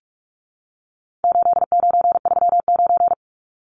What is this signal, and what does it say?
Morse code, heard as a keyed tone, about 44 wpm, 700 Hz. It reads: 8939
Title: Numbers - 4-Digit 45wpm